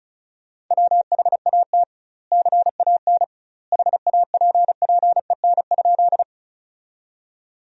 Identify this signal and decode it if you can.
Morse code, heard as a keyed tone, about 36 wpm, 700 Hz. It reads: WHAT CAN HAPPEN?
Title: 2x Word-Spacing - Sentences from Top 400 Words - 35wpm